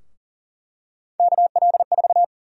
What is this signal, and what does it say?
Morse code, heard as a keyed tone, about 38 wpm, 700 Hz. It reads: KL4
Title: USA Call Sign Prefixes - Repeated 2x - 40wpm